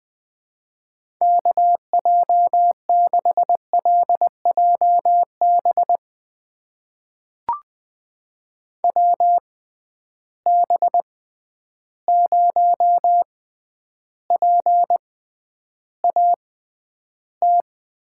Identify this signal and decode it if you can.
Morse code, heard as a keyed tone, about 20 wpm, spaced as for 10 wpm, 700 Hz. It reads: KJ6LJB WB0PAT
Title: USA Call Signs - 6 Characters - 6x Character Spacing - 20wpm